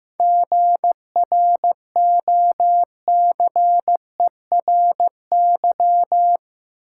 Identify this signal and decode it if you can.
Morse code, heard as a keyed tone, about 15 wpm, 700 Hz. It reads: GROCERY
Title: Seven Letter Words - 15wpm